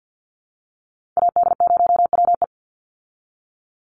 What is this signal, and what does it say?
Morse code, heard as a keyed tone, about 50 wpm, 700 Hz. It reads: AD0WE
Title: Call Signs Repeated 2x - 50wpm